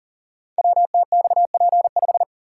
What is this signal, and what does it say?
Morse code, heard as a keyed tone, about 39 wpm, 700 Hz. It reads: WTXP5
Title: Code Groups - 5 Letter:Numbers - 40wpm